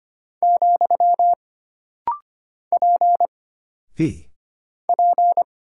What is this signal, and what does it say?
Morse code, heard as a keyed tone, about 25 wpm, 700 Hz. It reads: , P P
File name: Single Character (Letter:Number:Punctuation:Prosign) - Rapid-Fire - 25wpm